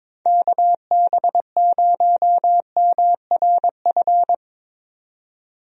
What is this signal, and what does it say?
Morse code, heard as a keyed tone, about 22 wpm, 700 Hz. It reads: KB0MRF
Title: USA Call Signs - 6 Characters - 22wpm